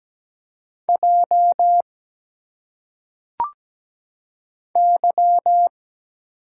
Morse code, heard as a keyed tone, about 17 wpm, 700 Hz. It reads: J Y
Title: Single Letters - Phonetic Pronunciation 17wpm